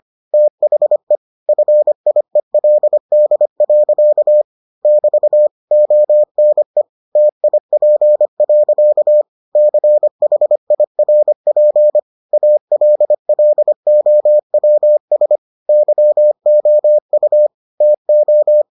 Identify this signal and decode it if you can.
Morse code, heard as a keyed tone, about 25 wpm, 600 Hz. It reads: THE FIELD. <BT> ONE TIP. CHIRP ALLOWS YOU TO